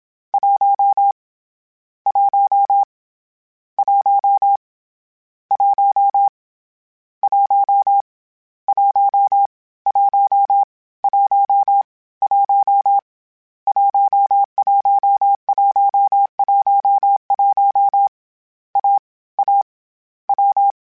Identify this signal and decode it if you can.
Morse code, heard as a keyed tone, about 27 wpm, 800 Hz. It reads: 1 1 1 1 1 1 1 1 1 11111 A A W
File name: Difficult Combos / Difficult Combos-mp3s / AWJ1 @25